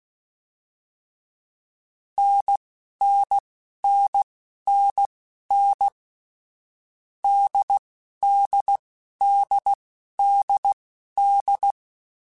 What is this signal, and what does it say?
Morse code, heard as a keyed tone, about 16 wpm, spaced as for 11 wpm, 800 Hz. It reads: NNNNN DDDDD
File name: Difficult Combos / Difficult Combos-mp3s / NDB NDB6 @10